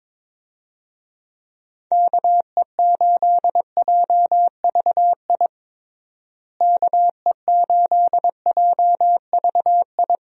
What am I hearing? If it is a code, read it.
Morse code, heard as a keyed tone, about 22 wpm, 700 Hz. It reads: KE8JVI KE8JVI
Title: USA Call Signs - 6 Characters - Repeated 2x - 22wpm